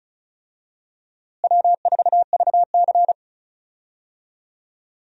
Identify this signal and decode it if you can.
Morse code, heard as a keyed tone, about 34 wpm, 700 Hz. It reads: W4VC
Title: USA Call Signs - 4 Characters - Repeated 3x - 35wpm